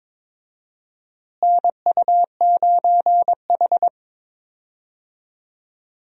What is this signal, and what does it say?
Morse code, heard as a keyed tone, about 22 wpm, 700 Hz. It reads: NU9H
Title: USA Call Signs - 4 Characters - Repeated 3x - 22wpm